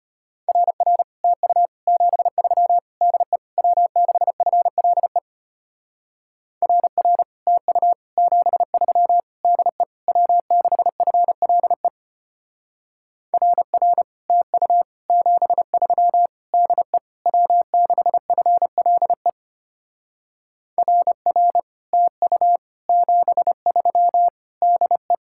Morse code, heard as a keyed tone, about 32 wpm, 700 Hz. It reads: RR TU 73 DE W6FLE RR TU 73 DE W6FLE RR TU 73 DE W6FLE RR TU 73 DE